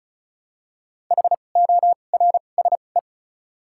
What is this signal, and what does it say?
Morse code, heard as a keyed tone, about 34 wpm, spaced as for 25 wpm, 700 Hz. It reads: HORSE
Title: 2x Character Spacing - Top 300 Words - 35wpm